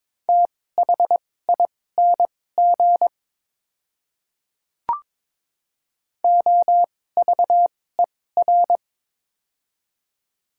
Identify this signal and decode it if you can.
Morse code, heard as a keyed tone, about 22 wpm, spaced as for 16 wpm, 700 Hz. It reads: THING OVER
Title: 2x Character Spacing - Top 200 Words - 22wpm